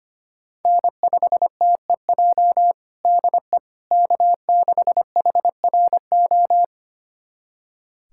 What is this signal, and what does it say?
Morse code, heard as a keyed tone, about 25 wpm, 700 Hz. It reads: N5TEJ DE K6HRO
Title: USA Callsign DE Callsign 25wpm